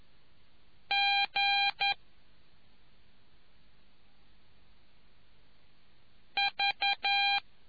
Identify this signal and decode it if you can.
Morse code, heard as a keyed tone, about 11 wpm, 2400 Hz. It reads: G V